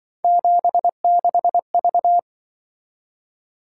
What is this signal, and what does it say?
Morse code, heard as a keyed tone, about 25 wpm, 700 Hz. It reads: 76V
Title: Speed-Racing - Three Letter-Number Combinations - 20wpm